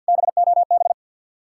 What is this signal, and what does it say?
Morse code, heard as a keyed tone, about 62 wpm, 700 Hz. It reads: BOB
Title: Speed-Racing Common QSO Phrases - 50wpm